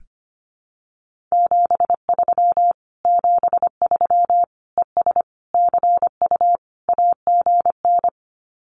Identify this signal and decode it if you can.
Morse code, heard as a keyed tone, about 25 wpm, 700 Hz. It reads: 73 73 ES CU AGN
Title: Common QSO Phrases Encore 25wpm